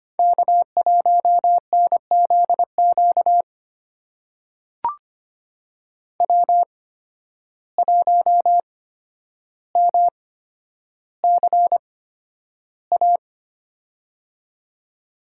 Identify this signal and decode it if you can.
Morse code, heard as a keyed tone, about 25 wpm, spaced as for 11 wpm, 700 Hz. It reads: K1NZQ W1MCA
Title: USA Call Signs - 5 Characters - 8x Character Spacing - 25wpm